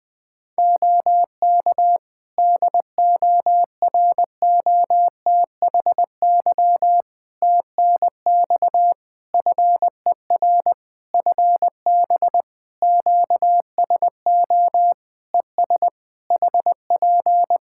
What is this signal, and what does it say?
Morse code, heard as a keyed tone, about 21 wpm, 700 Hz. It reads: OK DOROTHY TNX FER FB QSO ES HP
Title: Practice QSO - Exchange 4 - Ending - 20wpm